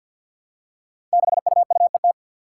Morse code, heard as a keyed tone, about 53 wpm, 700 Hz. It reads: 6UUET
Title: Five Challenging Letter-Number Combinations 50wpm